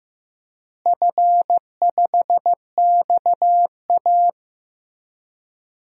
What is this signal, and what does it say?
Morse code, heard as a keyed tone, about 15 wpm, 700 Hz. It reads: F5XA
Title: DXCC-100 Call Signs - 4 Characters - Repeated 3x - 15wpm